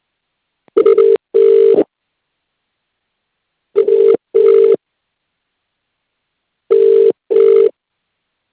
Ringback tone (UK), 400 + 450 Hz.